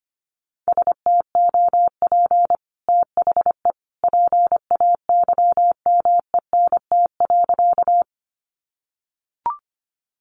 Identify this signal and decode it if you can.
Morse code, heard as a keyed tone, about 25 wpm, 700 Hz. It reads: STOP THE PAYMENT.
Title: Sentences from Top 1200 Words 25wpm